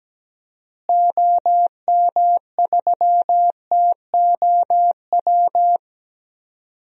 Morse code, heard as a keyed tone, about 17 wpm, 700 Hz. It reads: OM3TOW
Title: DXCC-100 Call Signs - 6 Characters - Repeated 2x - 17wpm